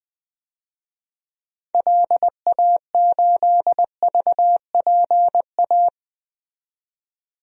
Morse code, heard as a keyed tone, about 20 wpm, 700 Hz. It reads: LA8VPA
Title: DXCC-100 Call Signs - 6 Characters - 20wpm